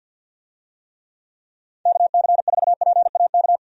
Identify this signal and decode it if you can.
Morse code, heard as a keyed tone, about 48 wpm, 700 Hz. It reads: KC4PAK
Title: USA Call Signs - 6 Characters - 50wpm